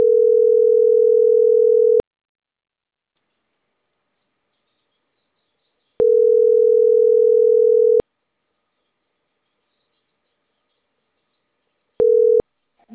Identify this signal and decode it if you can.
Ringback tone (North America), 440 + 480 Hz.